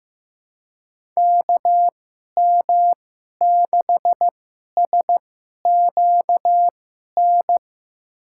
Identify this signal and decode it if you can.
Morse code, heard as a keyed tone, about 15 wpm, spaced as for 11 wpm, 700 Hz. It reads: KM6SQN
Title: USA Call Signs - 6 Characters - 2x Character Spacing - 15wpm